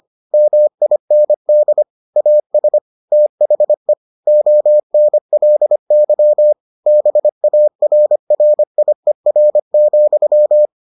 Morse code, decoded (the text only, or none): MIND AS THE ONLY BARRIER,